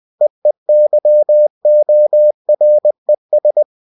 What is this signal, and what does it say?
Morse code, heard as a keyed tone, about 20 wpm, 600 Hz. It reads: EEYORES
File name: cw-books / winnie-the-pooh / winnie-20wpm-page-0050